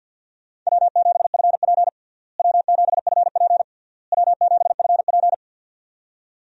Morse code, heard as a keyed tone, about 51 wpm, 700 Hz. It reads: W7FP W7FP W7FP
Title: USA Call Signs - 4 Characters - Repeated 3x - 50wpm